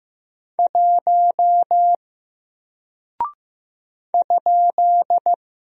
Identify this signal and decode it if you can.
Morse code, heard as a keyed tone, about 15 wpm, 700 Hz. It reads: 1 ?
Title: Single Character (Letter:Number:Punctuation:Prosign) Alt 15wpm